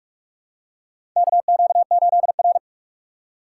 Morse code, heard as a keyed tone, about 46 wpm, 700 Hz. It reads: KQ8R
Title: USA Call Signs - 4x Character Spacing - 45wpm